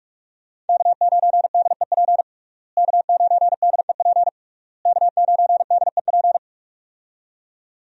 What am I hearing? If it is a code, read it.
Morse code, heard as a keyed tone, about 46 wpm, 700 Hz. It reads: K9DEP K9DEP K9DEP
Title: USA Call Signs - 5 Characters - Repeated 3x - 45wpm